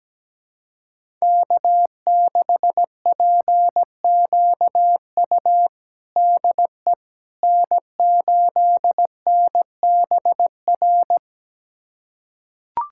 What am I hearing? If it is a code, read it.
Morse code, heard as a keyed tone, about 17 wpm, 700 Hz. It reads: K6PQU DE N8NBR
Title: USA Callsign DE Callsign - 5 Characters - 17wpm